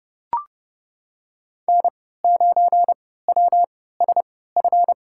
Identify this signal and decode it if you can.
Morse code, heard as a keyed tone, about 32 wpm, spaced as for 17 wpm, 700 Hz. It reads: N9WSF
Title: USA Call Signs - 5 Characters - 3x Character Spacing - 30wpm